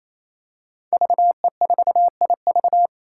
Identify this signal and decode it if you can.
Morse code, heard as a keyed tone, about 28 wpm, 700 Hz. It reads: VE4IV
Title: Five Challenging Letter-Number Combinations 28wpm